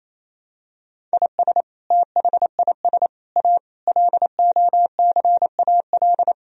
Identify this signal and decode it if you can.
Morse code, heard as a keyed tone, about 28 wpm, 700 Hz. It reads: IS THIS A LOCAL
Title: Sentences from Top 1000 Words - Review Part 3 - 28wpm